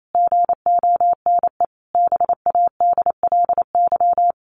Morse code, heard as a keyed tone, about 28 wpm, 700 Hz. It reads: GONE BADLY
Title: A Sound Of Thunder 28wpm